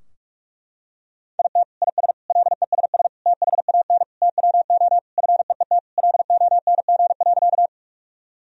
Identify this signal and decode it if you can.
Morse code, heard as a keyed tone, about 42 wpm, 700 Hz. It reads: IT IS LESS THAN TWO FEET LONG.
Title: Sentences from Top 1000 Words - Review Part 2 - 45wpm